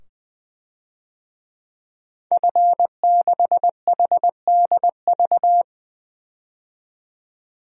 Morse code, heard as a keyed tone, about 20 wpm, 700 Hz. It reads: F6HDV
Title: DXCC-100 Call Signs - 5 Characters - 20wpm